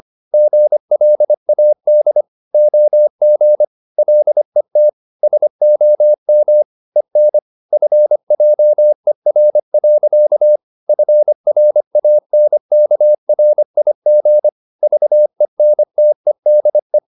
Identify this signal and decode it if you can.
Morse code, heard as a keyed tone, about 25 wpm, 600 Hz. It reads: GLAD OG LET SOM EN FJER. FRANKRIG VENTEDE
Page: Ferie 25 wpm